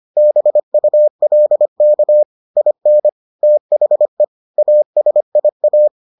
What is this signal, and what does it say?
Morse code, heard as a keyed tone, about 25 wpm, 600 Hz. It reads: BULK IN THE ASIA